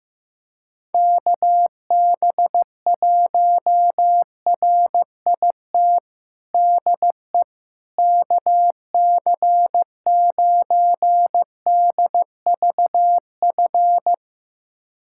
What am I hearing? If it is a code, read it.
Morse code, heard as a keyed tone, about 15 wpm, 700 Hz. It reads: KB1RIT DE KC9DVF
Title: USA Callsign DE Callsign - 6 Characters - 15wpm